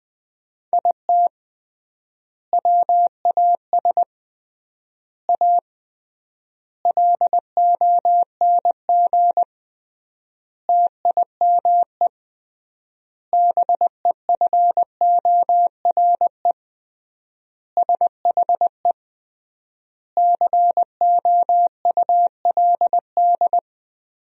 Morse code, heard as a keyed tone, about 20 wpm, 700 Hz. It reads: IT WAS A LONG TIME BEFORE SHE COULD
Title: 3x Word-Spacing - Sentences from Top 500 Words - 20wpm